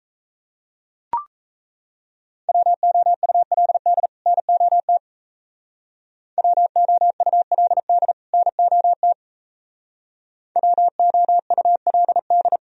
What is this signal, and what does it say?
Morse code, heard as a keyed tone, about 34 wpm, 700 Hz. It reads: WOULD NOT WOULD NOT WOULD
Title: Speed-Racing - Sets of 2 Words 28wpm